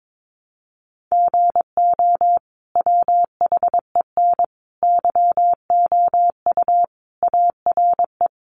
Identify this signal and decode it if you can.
Morse code, heard as a keyed tone, about 22 wpm, 700 Hz. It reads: GO WHEN YOU ARE
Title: Sentences from Top 500 Words 22wpm